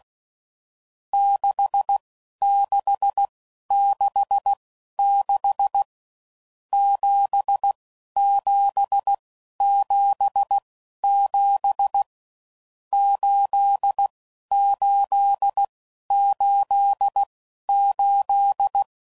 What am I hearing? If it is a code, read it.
Morse code, heard as a keyed tone, about 16 wpm, spaced as for 11 wpm, 800 Hz. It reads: 6666 7777 8888
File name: Mixed / Misc / Numbs